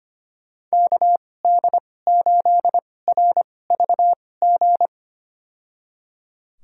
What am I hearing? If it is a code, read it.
Morse code, heard as a keyed tone, about 25 wpm, spaced as for 18 wpm, 700 Hz. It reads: KD8RVG
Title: USA Call Signs - 6 Characters - 2x Character Spacing - 25wpm